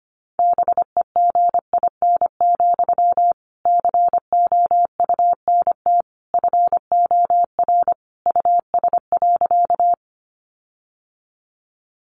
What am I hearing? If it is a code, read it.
Morse code, heard as a keyed tone, about 25 wpm, 700 Hz. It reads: BEGIN, COUNT FOR US.
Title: Sentences from Top 1500 Words 25wpm